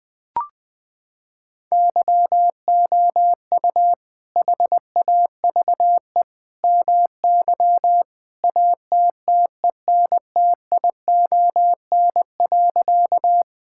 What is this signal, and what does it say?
Morse code, heard as a keyed tone, about 20 wpm, 700 Hz. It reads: YOU HAVE MY ATTENTION.
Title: Sentences from Top 1000 Words - Review Part 6 - 20wpm